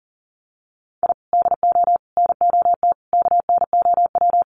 Morse code, heard as a keyed tone, about 40 wpm, 700 Hz. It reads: I DO NOT KNOW
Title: Sets of 5 Words Encore 40wpm